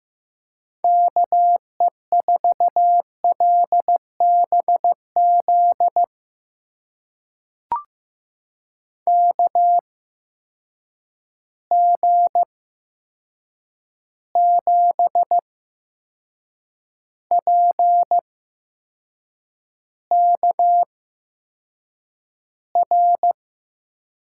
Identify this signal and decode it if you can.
Morse code, heard as a keyed tone, about 15 wpm, spaced as for 6 wpm, 700 Hz. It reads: KE4LBZ KG7PKR
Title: USA Call Signs - 6 Characters - 8x Character Spacing - 15wpm